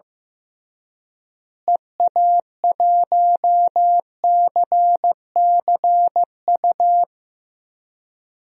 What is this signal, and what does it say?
Morse code, heard as a keyed tone, about 15 wpm, 700 Hz. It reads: EA1CCU
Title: DXCC-100 Call Signs - 6 Characters - Repeated 3x - 15wpm